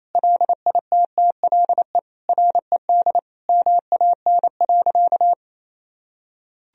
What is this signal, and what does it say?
Morse code, heard as a keyed tone, about 28 wpm, 700 Hz. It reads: LITTLE RED MAN.
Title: Sentences from Top 1000 Words - Review Part 2 - 28wpm